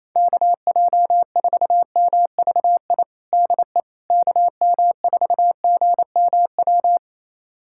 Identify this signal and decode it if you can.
Morse code, heard as a keyed tone, about 28 wpm, 700 Hz. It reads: KJ4MVI DE KM4GMW
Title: USA Callsign DE Callsign - 6 Characters - 28wpm